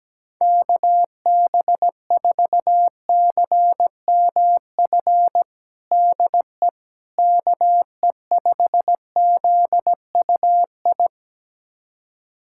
Morse code, heard as a keyed tone, about 17 wpm, 700 Hz. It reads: KB4CMF DE KE5ZUI
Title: USA Callsign DE Callsign 17wpm